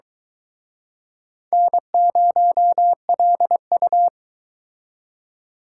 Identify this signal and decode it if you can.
Morse code, heard as a keyed tone, about 23 wpm, 700 Hz. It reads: N0LU